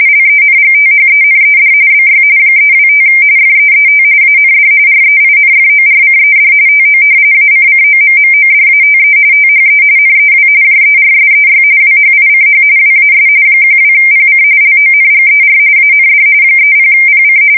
RTTY, about 45.45 baud. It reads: LENGTHENED BY 325 FEET AND REINFORCED TO ACCOMMODATE THE NEW JERSEY WHICH WAS 12PC LONGER AND 30PC
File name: fsk170 / battleshipnj